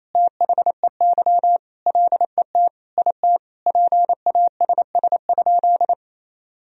Morse code, heard as a keyed tone, about 27 wpm, 700 Hz. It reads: THEY LET IT PASS?
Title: Sentences from Top 400 Words and Abbreviations 28wpm